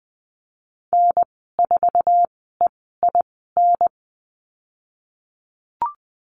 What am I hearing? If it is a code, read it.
Morse code, heard as a keyed tone, about 20 wpm, spaced as for 15 wpm, 700 Hz. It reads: N4EIN
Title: 048 Lesson 9g - Number 4 - Call Signs 20wpm